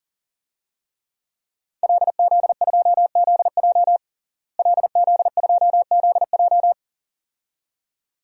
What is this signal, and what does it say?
Morse code, heard as a keyed tone, about 40 wpm, 700 Hz. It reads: LZ2ZJ LZ2ZJ
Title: DXCC-100 Call Signs - 5 Characters - Repeated 2x - 40wpm